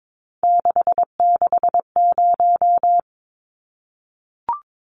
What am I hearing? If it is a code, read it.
Morse code, heard as a keyed tone, about 22 wpm, 700 Hz. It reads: 660
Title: Numbers - 3-Digit 22wpm